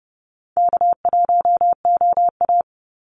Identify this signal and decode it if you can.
Morse code, heard as a keyed tone, about 30 wpm, 700 Hz. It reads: K1OA
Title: Call Signs Repeated 3x - 30wpm